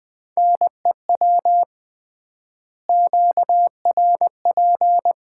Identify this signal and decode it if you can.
Morse code, heard as a keyed tone, about 21 wpm, 700 Hz. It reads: NEW QRP
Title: Common QSO Phrases - 3x Word-Spacing - 20wpm